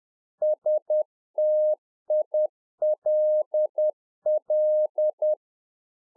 Morse code, heard as a keyed tone, about 10 wpm, 610 Hz. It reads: STILL
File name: UNFOGETTABLE QSO 10WPM